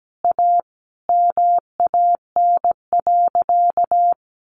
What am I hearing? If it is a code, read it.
Morse code, heard as a keyed tone, about 17 wpm, 700 Hz. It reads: A MAN.
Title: Sentences from Top 1400 Words 17wpm